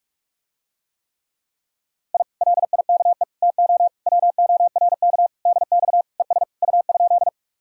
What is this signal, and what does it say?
Morse code, heard as a keyed tone, about 44 wpm, 700 Hz. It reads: I LIKE TO WORK DX ES U?